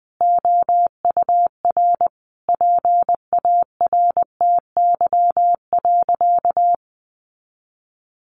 Morse code, heard as a keyed tone, about 20 wpm, 700 Hz. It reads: OUR PARTY.
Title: Sentences from Top 1400 Words 20wpm